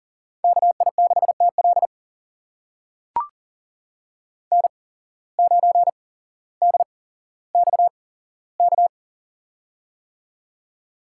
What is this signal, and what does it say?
Morse code, heard as a keyed tone, about 40 wpm, spaced as for 17 wpm, 700 Hz. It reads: KI6TL N9DXK
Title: USA Call Signs - 5 Characters - 8x Character Spacing - 40wpm